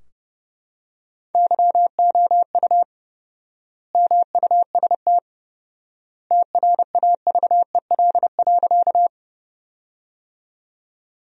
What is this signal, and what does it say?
Morse code, heard as a keyed tone, about 32 wpm, 700 Hz. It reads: YOU MUST TRAVEL.
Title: 4x Word-Spacing - Sentences from Top 400 Words - 30wpm